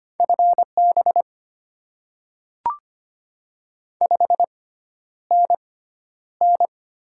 Morse code, heard as a keyed tone, about 25 wpm, spaced as for 11 wpm, 700 Hz. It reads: FB 5NN
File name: Top 20 QSO Elements - 6x Character Spacing - 25wpm